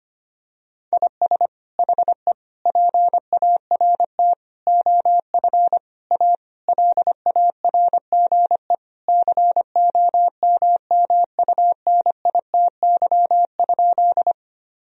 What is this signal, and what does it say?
Morse code, heard as a keyed tone, about 25 wpm, 700 Hz. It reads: IS HE PART OF A LARGE COMMUNITY?
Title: Sentences from Top 1000 Words - Review Part 4 - 25wpm